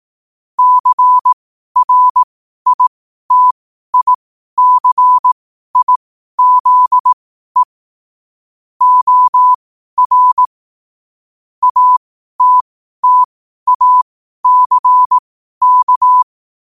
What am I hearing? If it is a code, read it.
Morse code, heard as a keyed tone, about 18 wpm, spaced as for 12 wpm, 1000 Hz. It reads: CRITICIZE OR ATTACK
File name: qotdfiles / 12wpm